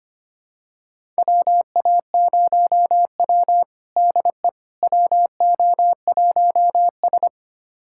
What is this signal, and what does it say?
Morse code, heard as a keyed tone, about 25 wpm, 700 Hz. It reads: WA0W DE WO1S
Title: USA Callsign DE Callsign - 4 Characters - 25wpm